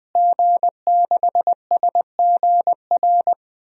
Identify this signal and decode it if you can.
Morse code, heard as a keyed tone, about 20 wpm, 700 Hz. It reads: G6SGR
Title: DXCC-100 Call Signs - 5 Characters - Repeated 3x - 20wpm